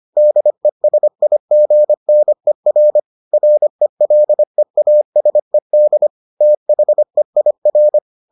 Morse code, heard as a keyed tone, about 25 wpm, 600 Hz. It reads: DESIGNER RELEASED THEIR